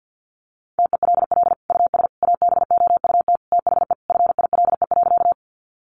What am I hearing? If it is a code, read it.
Morse code, heard as a keyed tone, about 50 wpm, 700 Hz. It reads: TELL US ABOUT THE FILE.
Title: Sentences from Top 1200 Words 50wpm